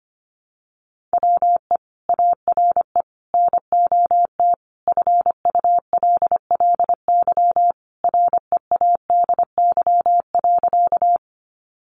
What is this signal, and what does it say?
Morse code, heard as a keyed tone, about 25 wpm, 700 Hz. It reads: WE ARE NOT FULLY READY.
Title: Sentences from Top 1300 Words 25wpm